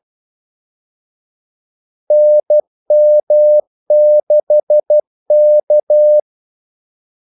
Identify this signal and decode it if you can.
Morse code, heard as a keyed tone, about 12 wpm, 600 Hz. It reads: NM6K
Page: 2 ekstra ordmellemrum mellem kaldesignalerne